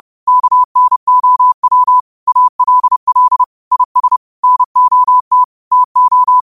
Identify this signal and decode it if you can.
Morse code, heard as a keyed tone, about 30 wpm, 1000 Hz. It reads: KNOW ALL IS NOT TO
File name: qotdfiles / 30wpm